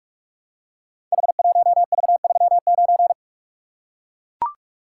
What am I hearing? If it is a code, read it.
Morse code, heard as a keyed tone, about 44 wpm, 700 Hz. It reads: H1V39
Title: Code Groups - 5 Letter:Numbers - 45wpm